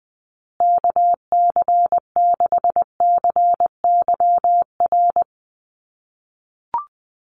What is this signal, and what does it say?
Morse code, heard as a keyed tone, about 20 wpm, 700 Hz. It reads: KC6CYR
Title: USA Call Signs Repeated 2x 20wpm